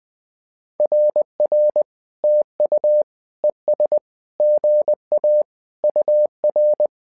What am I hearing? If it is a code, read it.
Morse code, heard as a keyed tone, about 20 wpm, 600 Hz. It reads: RR TU ES GA UR
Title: SOTA QSO Practice 20wpm